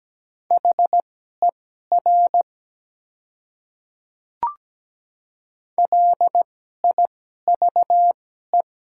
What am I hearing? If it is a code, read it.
Morse code, heard as a keyed tone, about 17 wpm, spaced as for 12 wpm, 700 Hz. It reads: HER LIVE
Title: 2x Character Spacing - Top 200 Words - 17wpm